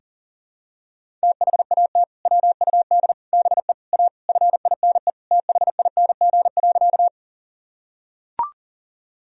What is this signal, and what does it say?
Morse code, heard as a keyed tone, about 40 wpm, 700 Hz. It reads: THAT WUD BE A FINE THING.
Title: Sentences from Top 500 Words and Abbreviations 40wpm